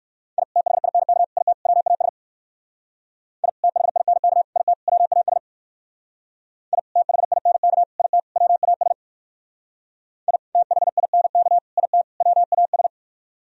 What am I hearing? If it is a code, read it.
Morse code, heard as a keyed tone, about 57 wpm, 700 Hz. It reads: I THINK IT WAS I THINK IT WAS I THINK IT WAS I THINK IT WAS